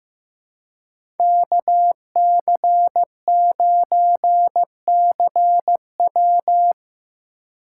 Morse code, heard as a keyed tone, about 15 wpm, 700 Hz. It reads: KC9CW
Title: USA Call Signs - 2x Character Spacing - 15wpm